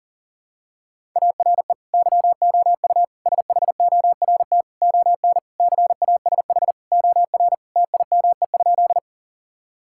Morse code, heard as a keyed tone, about 40 wpm, 700 Hz. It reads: ARE YOU SHORT ON CASH OR TIME?
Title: Sentences from Top 1000 Words - Review Part 5 - 40wpm